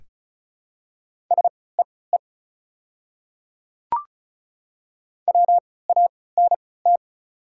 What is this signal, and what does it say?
Morse code, heard as a keyed tone, about 36 wpm, spaced as for 20 wpm, 700 Hz. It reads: SEE WANT
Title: 3x Character Spacing - Top 200 Words - 35wpm